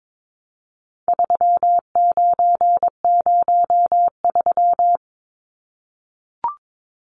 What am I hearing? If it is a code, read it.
Morse code, heard as a keyed tone, about 22 wpm, 700 Hz. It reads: 3903
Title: Numbers - 4-Digit 22wpm